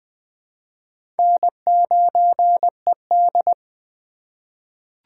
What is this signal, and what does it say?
Morse code, heard as a keyed tone, about 20 wpm, 700 Hz. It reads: N9ED
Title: USA Call Signs - 4 Characters - 4x Character Spacing - 20wpm